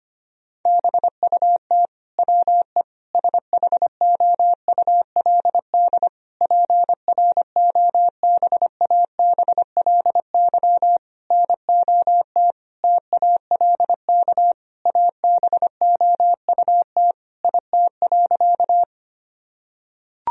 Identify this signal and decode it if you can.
Morse code, heard as a keyed tone, about 25 wpm, 700 Hz. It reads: BUT WE SHOULD PROBABLY NOT TALK ABOUT IT.
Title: Sentences from Top 1000 Words - Review Part 4 - 25wpm